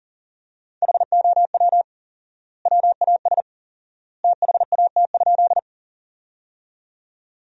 Morse code, heard as a keyed tone, about 40 wpm, 700 Hz. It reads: HOW WAS THAT?
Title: 4x Word-Spacing - Sentences from Top 100 Words - 40wpm